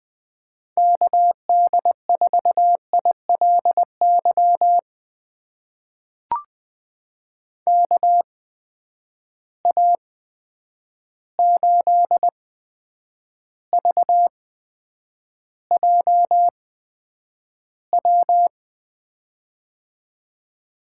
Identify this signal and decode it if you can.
Morse code, heard as a keyed tone, about 20 wpm, spaced as for 9 wpm, 700 Hz. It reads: KD4ILY KA8VJW
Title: USA Call Signs - 6 Characters - 8x Character Spacing - 20wpm